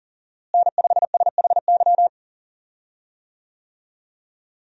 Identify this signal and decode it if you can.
Morse code, heard as a keyed tone, about 40 wpm, 700 Hz. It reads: N5SHY
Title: USA Call Signs - 5 Characters - Repeated 2x - 40wpm